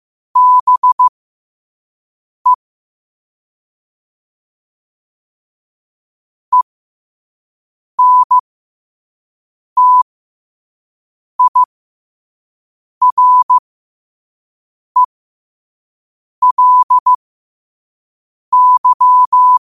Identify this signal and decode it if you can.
Morse code, heard as a keyed tone, about 15 wpm, spaced as for 5 wpm, 1000 Hz. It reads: BE ENTIRELY
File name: qotdfiles / 5wpm